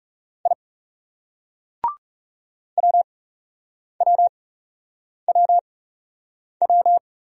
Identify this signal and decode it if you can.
Morse code, heard as a keyed tone, about 34 wpm, spaced as for 8 wpm, 700 Hz. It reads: I WWWW
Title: Speed-Racing Single Letter-Number 30wpm